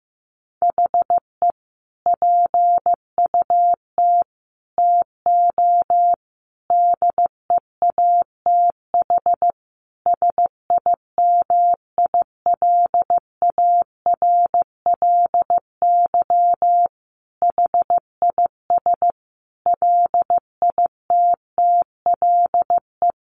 Morse code, heard as a keyed tone, about 15 wpm, 700 Hz. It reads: HE PUT TO DEATH SIMILARLY HIS LITTLE
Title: Sentences from Top 1500 Words 15wpm